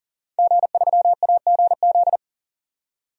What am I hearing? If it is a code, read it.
Morse code, heard as a keyed tone, about 40 wpm, 700 Hz. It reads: G3AGZ
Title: DXCC-100 Call Signs - 5 Characters - 40wpm